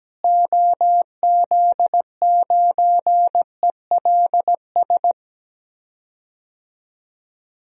Morse code, heard as a keyed tone, about 17 wpm, 700 Hz. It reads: OZ9ELS
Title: DXCC-100 Call Signs - 6 Characters - Repeated 2x - 17wpm